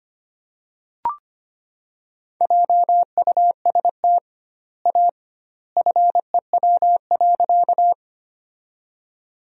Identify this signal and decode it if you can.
Morse code, heard as a keyed tone, about 25 wpm, 700 Hz. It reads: JUST A FEW.
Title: 2x Word-Spacing - Sentences from Top 300 Words - 25wpm